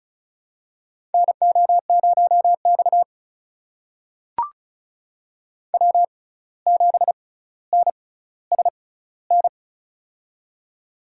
Morse code, heard as a keyed tone, about 36 wpm, spaced as for 17 wpm, 700 Hz. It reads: NO0X W7NSN
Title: USA Call Signs - 6x Character Spacing - 35wpm